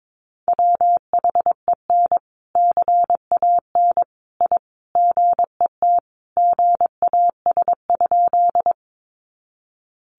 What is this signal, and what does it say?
Morse code, heard as a keyed tone, about 22 wpm, 700 Hz. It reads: WHEN CAN I GET GAS?
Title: Sentences from Top 1100 Words 22wpm